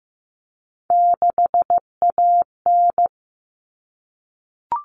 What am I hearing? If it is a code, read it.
Morse code, heard as a keyed tone, about 15 wpm, 700 Hz. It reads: 6AN
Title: Common Cut Numbers - 3-Digit 15wpm